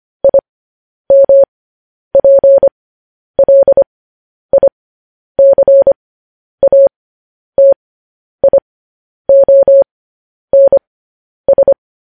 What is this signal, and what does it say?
Morse code, heard as a keyed tone, about 25 wpm, spaced as for 10 wpm, 550 Hz. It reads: IMPLICATIONS